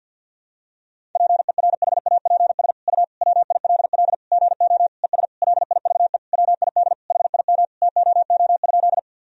Morse code, heard as a keyed tone, about 52 wpm, 700 Hz. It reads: PERHAPS U WILL GO ES LIVE WID HIM TOO?
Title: Sentences from Top 500 Words and Abbreviations 50wpm